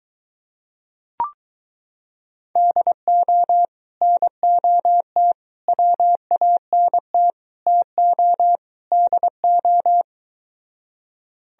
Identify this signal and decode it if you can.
Morse code, heard as a keyed tone, about 23 wpm, 700 Hz. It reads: DO NOT WANT TO DO